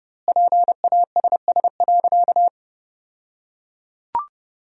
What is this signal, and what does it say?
Morse code, heard as a keyed tone, about 30 wpm, 700 Hz. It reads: PASS.
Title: 2x Word-Spacing - Sentences from Top 400 Words - 30wpm